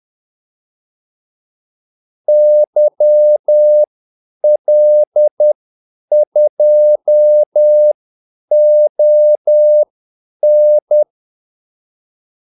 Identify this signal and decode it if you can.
Morse code, heard as a keyed tone, about 10 wpm, spaced as for 8 wpm, 600 Hz. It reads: YL2ON
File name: Callsigns-10-8wpm